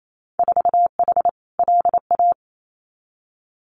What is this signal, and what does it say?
Morse code, heard as a keyed tone, about 28 wpm, 700 Hz. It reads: 4H LA
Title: Winter Field Day Exchanges Encore 28wpm